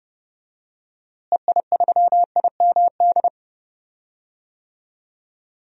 Morse code, heard as a keyed tone, about 31 wpm, 700 Hz. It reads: EI3IMD
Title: DXCC-100 Call Signs - 6 Characters - Repeated 2x - 30wpm